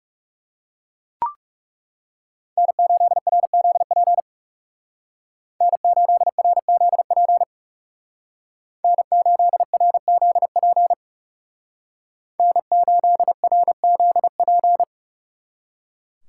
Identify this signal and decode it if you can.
Morse code, heard as a keyed tone, about 35 wpm, 700 Hz. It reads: N8RZP N8RZP N8RZP N8RZP
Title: Speed Racing - USA 1x3 Call Signs 30wpm